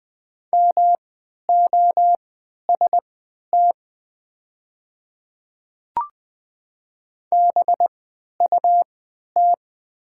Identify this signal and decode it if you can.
Morse code, heard as a keyed tone, about 20 wpm, spaced as for 11 wpm, 700 Hz. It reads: MOST BUT
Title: 3x Character Spacing - Top 100 Words - 20wpm